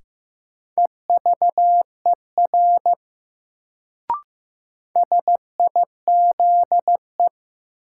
Morse code, heard as a keyed tone, about 15 wpm, 700 Hz. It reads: EVER SIZE
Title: Sentences from Top 400 Words - New Word Drill - 15wpm